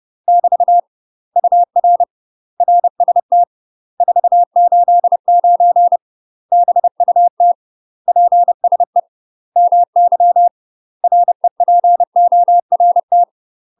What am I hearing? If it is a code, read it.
Morse code, heard as a keyed tone, about 30 wpm, 700 Hz. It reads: <BT> UR RST 489 BUT PSE MY REPORT
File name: UK MASTER QSOs TRAINING / 30WPM / UK MASTER QSOs 0023